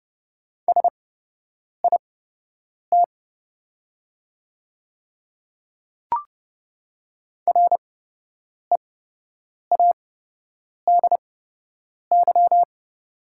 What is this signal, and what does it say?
Morse code, heard as a keyed tone, about 30 wpm, spaced as for 8 wpm, 700 Hz. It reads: SIT READY
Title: 8x Character Spacing - Top 500 Words - 30wpm